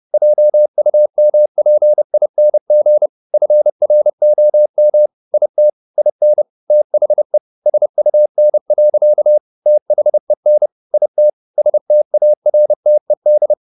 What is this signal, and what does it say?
Morse code, heard as a keyed tone, about 30 wpm, 600 Hz. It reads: JUMPING FROM IT IN THE SUN. THEN IT STARTED